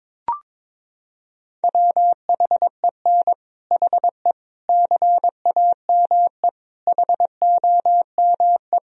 Morse code, heard as a keyed tone, about 21 wpm, 700 Hz. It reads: WHEN HE CAME HOME
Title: Sentences from Top 1000 Words - Review Part 3 - 22wpm